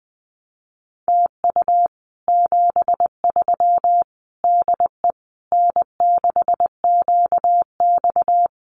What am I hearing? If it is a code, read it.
Morse code, heard as a keyed tone, about 20 wpm, 700 Hz. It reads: TU 73 DE N6QX
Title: Common QSO Phrases Encore 20wpm